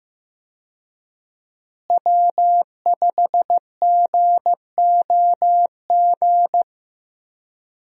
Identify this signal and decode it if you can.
Morse code, heard as a keyed tone, about 15 wpm, 700 Hz. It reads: W5GOG
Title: USA Call Signs - 5 Characters - Repeated 2x - 15wpm